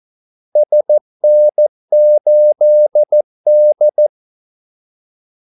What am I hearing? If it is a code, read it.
Morse code, heard as a keyed tone, about 14 wpm, 600 Hz. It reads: SN8D
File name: Callsigns-14wpm